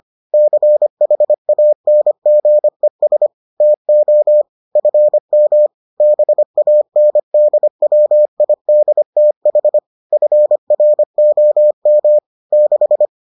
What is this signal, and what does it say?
Morse code, heard as a keyed tone, about 25 wpm, 600 Hz. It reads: CHANGES TO FM BANDWIDTH FROM 6